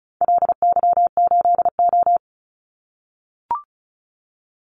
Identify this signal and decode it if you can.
Morse code, heard as a keyed tone, about 36 wpm, 700 Hz. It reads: LY8O
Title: Call Signs - 35wpm